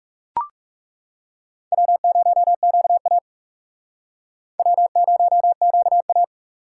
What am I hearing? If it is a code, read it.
Morse code, heard as a keyed tone, about 37 wpm, 700 Hz. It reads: W0QA W0QA
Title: Speed Racing - USA 1x2 Call Signs 30wpm